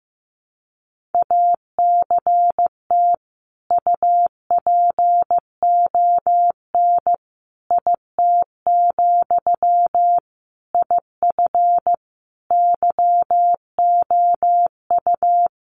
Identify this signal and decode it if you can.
Morse code, heard as a keyed tone, about 15 wpm, 700 Hz. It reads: ACT UPON IT, IF YOU
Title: Sentences from Top 1100 Words 15wpm